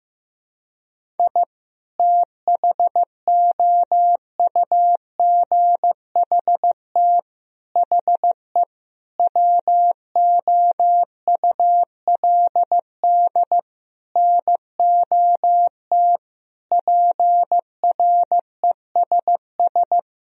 Morse code, heard as a keyed tone, about 15 wpm, 700 Hz. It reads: I THOUGHT HE WOULD NOT PRESS
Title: Sentences from Top 1000 Words - Review Part 1 - 15wpm